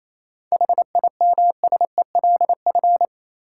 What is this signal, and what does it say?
Morse code, heard as a keyed tone, about 28 wpm, 700 Hz. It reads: HIMSELF
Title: Sentences from Top 1500 Words - New Word Drill Alt - 28wpm